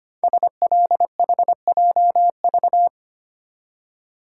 Morse code, heard as a keyed tone, about 25 wpm, 700 Hz. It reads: SLHJV
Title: Code Groups - 5 Letters - 25wpm